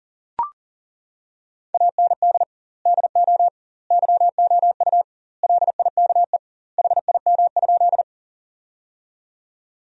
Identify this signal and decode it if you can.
Morse code, heard as a keyed tone, about 37 wpm, 700 Hz. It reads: AND DO YOU LIKE HIM?
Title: 2x Word-Spacing - Sentences from Top 100 Words - 40wpm